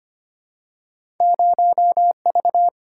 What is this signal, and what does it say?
Morse code, heard as a keyed tone, about 25 wpm, 700 Hz. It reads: 0V
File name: Two Letter-Number Combinations - 25wpm